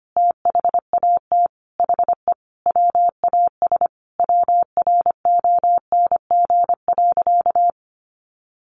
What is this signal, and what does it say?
Morse code, heard as a keyed tone, about 25 wpm, 700 Hz. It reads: THAT HE WAS WRONG.
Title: Sentences from Top 1100 Words 25wpm